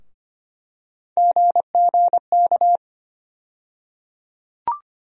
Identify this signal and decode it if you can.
Morse code, heard as a keyed tone, about 25 wpm, 700 Hz. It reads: GGK
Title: USA Call Sign Postfixes - 3 Letters 25wpm